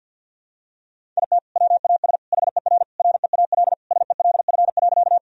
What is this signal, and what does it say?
Morse code, heard as a keyed tone, about 48 wpm, 700 Hz. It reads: IT WAS HER REAL SELF.
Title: Sentences from Top 1000 Words - Review Part 1 - 50wpm